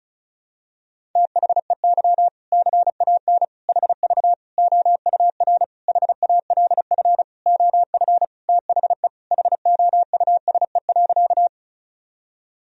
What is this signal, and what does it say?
Morse code, heard as a keyed tone, about 36 wpm, 700 Hz. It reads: THEY CAN HV OUR HALF OF THE HOUSE.
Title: Sentences from Top 400 Words and Abbreviations 35wpm